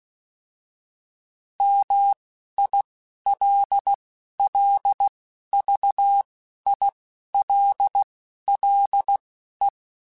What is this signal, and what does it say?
Morse code, heard as a keyed tone, about 16 wpm, spaced as for 11 wpm, 800 Hz. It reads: MILLVILLE